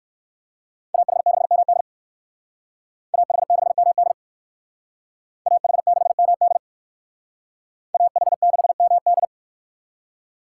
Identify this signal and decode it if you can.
Morse code, heard as a keyed tone, about 60 wpm, 700 Hz. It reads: AH6MD AH6MD AH6MD AH6MD